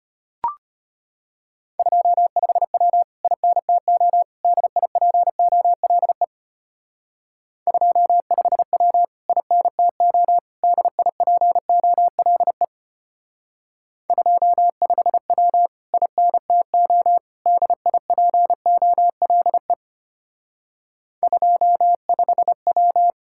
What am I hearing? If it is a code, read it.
Morse code, heard as a keyed tone, about 32 wpm, 700 Hz. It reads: 25W INTO DIPOLE 25W INTO DIPOLE 25W INTO DIPOLE 25W